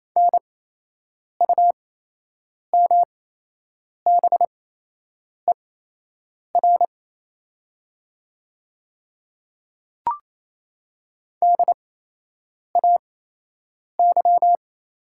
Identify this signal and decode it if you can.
Morse code, heard as a keyed tone, about 28 wpm, spaced as for 8 wpm, 700 Hz. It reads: NUMBER DAY
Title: 8x Character Spacing - Top 500 Words - 28wpm